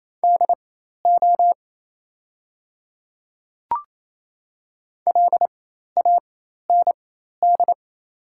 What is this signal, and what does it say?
Morse code, heard as a keyed tone, about 28 wpm, spaced as for 13 wpm, 690 Hz. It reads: DO LAND
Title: 4x Character Spacing - Top 200 Words - 28wpm